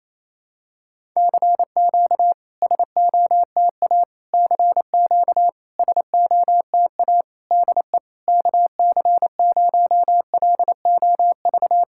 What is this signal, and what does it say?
Morse code, heard as a keyed tone, about 28 wpm, 700 Hz. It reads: CQ SOTA CQ SOTA DE KC0LOV
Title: SOTA QSO Practice 28wpm